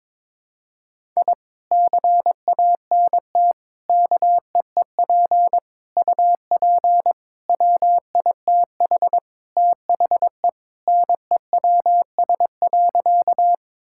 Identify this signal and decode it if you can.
Morse code, heard as a keyed tone, about 22 wpm, 700 Hz. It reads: I CANT KEEP UP WITH THE NEWS.
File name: practice / 22wpm / Sentences with 2-Word Phrases - 22wpm